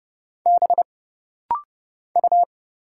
Morse code, heard as a keyed tone, about 29 wpm, 700 Hz. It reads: B U
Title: Single Letters - Rapid-Fire 30wpm